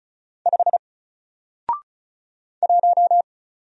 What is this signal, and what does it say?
Morse code, heard as a keyed tone, about 34 wpm, 700 Hz. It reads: 5 1
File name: Single-Letter-Number-35wpm